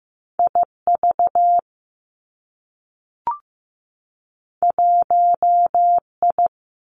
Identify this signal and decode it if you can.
Morse code, heard as a keyed tone, about 15 wpm, 700 Hz. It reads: I V 1 I
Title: Two Challenging Letter-Number Combinations 15wpm